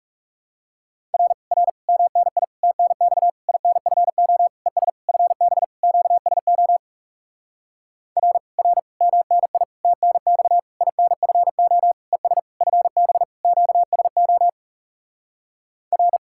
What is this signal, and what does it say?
Morse code, heard as a keyed tone, about 37 wpm, 700 Hz. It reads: R R MNI TNX INFO ES FB QSO R R MNI TNX INFO ES FB QSO R
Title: Speed-Racing Common QSO Phrases - 30wpm